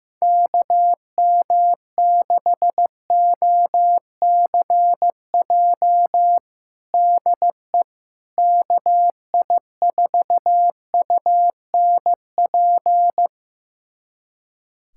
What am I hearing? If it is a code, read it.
Morse code, heard as a keyed tone, about 15 wpm, 700 Hz. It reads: KM6OCJ DE KI4UNP
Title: USA Callsign DE Callsign - 6 Characters - 15wpm